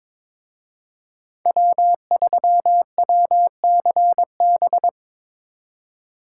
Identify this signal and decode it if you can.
Morse code, heard as a keyed tone, about 26 wpm, 700 Hz. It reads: W3WCB
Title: Speed Racing - USA 1x3 Call Signs 22wpm